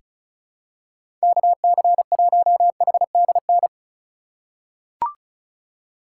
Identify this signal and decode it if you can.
Morse code, heard as a keyed tone, about 34 wpm, 700 Hz. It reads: KC1HDN
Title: USA Call Signs - 2x Character Spacing - 35wpm